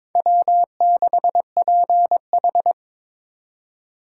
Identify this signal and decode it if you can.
Morse code, heard as a keyed tone, about 22 wpm, 700 Hz. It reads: W6PH
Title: USA Call Signs - 4 Characters - 4x Character Spacing - 22wpm